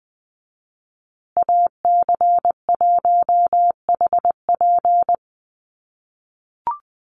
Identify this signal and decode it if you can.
Morse code, heard as a keyed tone, about 20 wpm, 700 Hz. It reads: AC1HP
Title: USA Call Signs Repeated 2x 20wpm